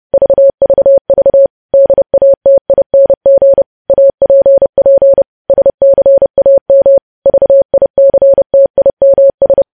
Morse code, heard as a keyed tone, about 30 wpm, 550 Hz. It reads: VVV DATING APP SCAM VICTIMS
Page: news headlines in morse code at 30 wpm